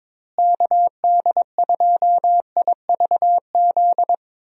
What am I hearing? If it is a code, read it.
Morse code, heard as a keyed tone, about 22 wpm, 700 Hz. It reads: KD2IVZ
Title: USA Call Signs - 6 Characters - Repeated 2x - 22wpm